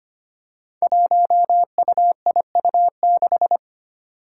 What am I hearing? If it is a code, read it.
Morse code, heard as a keyed tone, about 25 wpm, 700 Hz. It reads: 1UIU6
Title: Five Challenging Letter-Number Combinations 25wpm